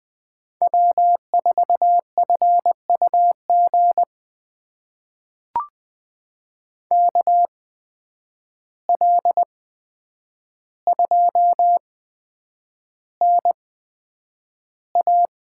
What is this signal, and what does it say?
Morse code, heard as a keyed tone, about 20 wpm, spaced as for 9 wpm, 700 Hz. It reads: W4FUG KL2NA
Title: USA Call Signs - 8x Character Spacing - 20wpm